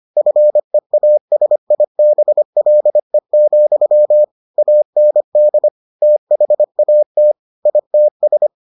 Morse code, heard as a keyed tone, about 25 wpm, 600 Hz. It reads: FEASIBLE, AND THAT ITS